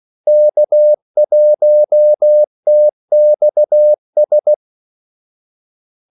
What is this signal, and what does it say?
Morse code, heard as a keyed tone, about 16 wpm, 600 Hz. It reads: K1TXS
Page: Kaldesignaler 16 wpm